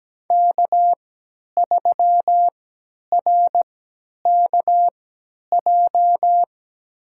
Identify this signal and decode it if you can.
Morse code, heard as a keyed tone, about 17 wpm, spaced as for 10 wpm, 700 Hz. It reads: K3RKJ
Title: USA Call Signs - 5 Characters - 3x Character Spacing - 17wpm